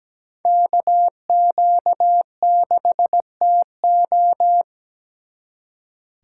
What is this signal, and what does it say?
Morse code, heard as a keyed tone, about 20 wpm, 700 Hz. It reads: KQ6TO
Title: Speed Racing - USA 2x2 Call Signs 17wpm